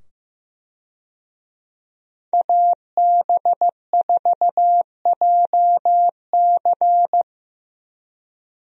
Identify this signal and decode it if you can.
Morse code, heard as a keyed tone, about 15 wpm, 700 Hz. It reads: AB4JC
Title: USA Call Signs - 5 Characters - 15wpm